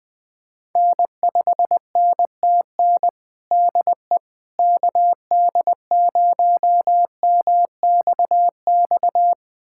Morse code, heard as a keyed tone, about 21 wpm, 700 Hz. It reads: N5NTN DE KD0MXX
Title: USA Callsign DE Callsign 20wpm